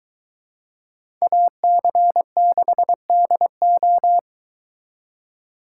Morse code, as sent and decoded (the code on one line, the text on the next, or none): .- -.-. -.... -.. ---
AC6DO